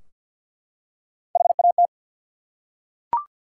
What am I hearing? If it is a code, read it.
Morse code, heard as a keyed tone, about 50 wpm, 700 Hz. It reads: HAT
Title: Three Letter Words - 50wpm